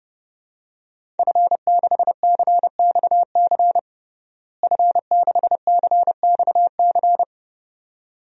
Morse code, heard as a keyed tone, about 30 wpm, 700 Hz. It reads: F6CXC F6CXC
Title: DXCC-100 Call Signs - 5 Characters - Repeated 2x - 30wpm